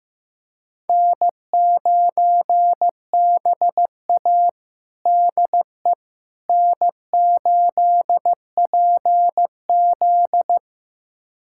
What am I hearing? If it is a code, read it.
Morse code, heard as a keyed tone, about 15 wpm, 700 Hz. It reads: N9BA DE N8PZ
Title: USA Callsign DE Callsign - 4 Characters - 15wpm